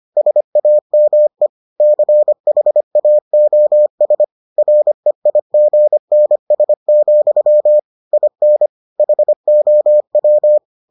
Morse code, heard as a keyed tone, about 25 wpm, 600 Hz. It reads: SAME CHAOS REIGNS, IN HOW